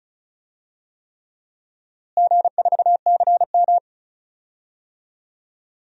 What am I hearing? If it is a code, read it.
Morse code, heard as a keyed tone, about 36 wpm, 700 Hz. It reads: G4CM
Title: DXCC-100 Call Signs - 4 Characters - Repeated 3x - 35wpm